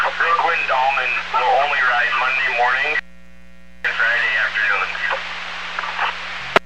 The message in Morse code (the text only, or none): none